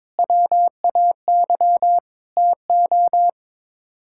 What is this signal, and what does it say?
Morse code, heard as a keyed tone, about 22 wpm, 700 Hz. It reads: WAY TO
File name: practice / 22wpm / Sets of 6 Words - 22wpm